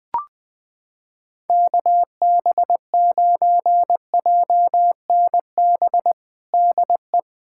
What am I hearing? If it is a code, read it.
Morse code, heard as a keyed tone, about 20 wpm, 700 Hz. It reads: KB9JNB DE
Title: USA Callsign DE Callsign 20wpm